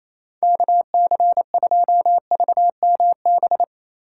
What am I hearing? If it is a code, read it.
Morse code, heard as a keyed tone, about 26 wpm, 700 Hz. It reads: KC2VMB
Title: Speed Racing - USA 2x3 Call Signs 28wpm